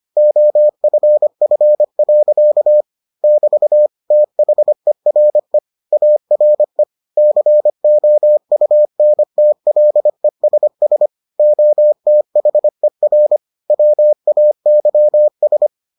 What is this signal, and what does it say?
Morse code, heard as a keyed tone, about 25 wpm, 600 Hz. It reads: OFF. <BT> THERE ARE COUNTLESS OTHER WAYS